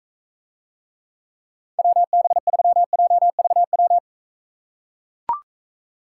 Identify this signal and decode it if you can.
Morse code, heard as a keyed tone, about 41 wpm, 700 Hz. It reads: WB3JVW